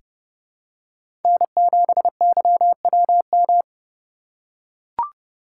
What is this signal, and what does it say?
Morse code, heard as a keyed tone, about 32 wpm, 700 Hz. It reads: N7YWM
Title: USA Call Signs - 5 Characters - 2x Character Spacing - 30wpm